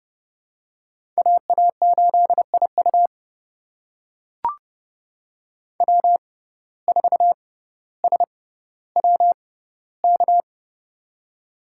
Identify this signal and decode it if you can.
Morse code, heard as a keyed tone, about 30 wpm, spaced as for 15 wpm, 700 Hz. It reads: AA8IU W4SWK
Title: USA Call Signs - 5 Characters - 6x Character Spacing - 30wpm